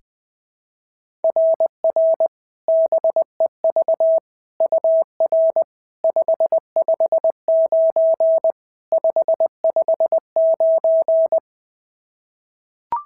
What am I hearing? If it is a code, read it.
Morse code, heard as a keyed tone, about 20 wpm, 660 Hz. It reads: RR BEV UR 559 559
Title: SOTA QSO Practice 20wpm